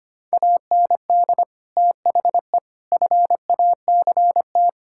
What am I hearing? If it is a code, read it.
Morse code, heard as a keyed tone, about 25 wpm, 700 Hz. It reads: AND THE FACT
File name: practice / 25wpm / Sentences with 4-Word Phrases - 25wpm